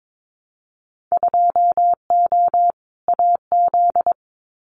Morse code, heard as a keyed tone, about 22 wpm, 700 Hz. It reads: 2O AZ
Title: Winter Field Day Exchanges 22wpm